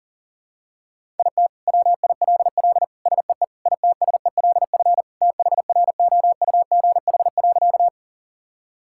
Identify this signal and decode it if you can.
Morse code, heard as a keyed tone, about 40 wpm, 700 Hz. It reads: IT WILL SEE ITSELF THROUGH.
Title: Sentences from Top 1000 Words - Review Part 5 - 40wpm